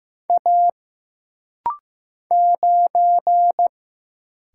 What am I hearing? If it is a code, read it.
Morse code, heard as a keyed tone, about 15 wpm, 700 Hz. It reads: A 9
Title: Single Letter-Number Alt - Rapid-Fire 15wpm